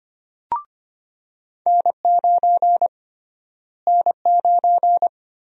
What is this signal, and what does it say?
Morse code, heard as a keyed tone, about 25 wpm, 700 Hz. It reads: N9 N9
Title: USA Call Sign Prefixes - Repeated 2x - 25wpm